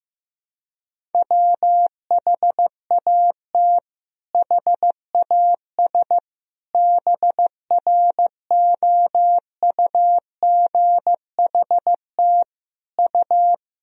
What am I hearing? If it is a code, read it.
Morse code, heard as a keyed tone, about 15 wpm, 700 Hz. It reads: WHAT HAS BROUGHT U
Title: Sentences from Top 500 Words and Abbreviations 15wpm